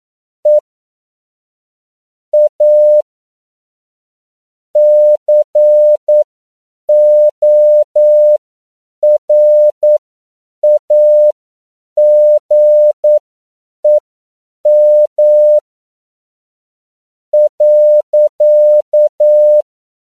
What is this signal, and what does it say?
Morse code, heard as a keyed tone, about 9 wpm, spaced as for 6 wpm, 600 Hz. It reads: E A CORAGEM.